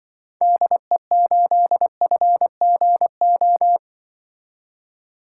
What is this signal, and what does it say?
Morse code, heard as a keyed tone, about 22 wpm, 700 Hz. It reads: DE8FGO
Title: Speed Racing - USA 2x3 Call Signs 20wpm